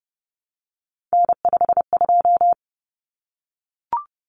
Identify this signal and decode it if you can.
Morse code, heard as a keyed tone, about 30 wpm, 700 Hz. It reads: N52
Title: Common Cut Numbers - 3-Digit 30wpm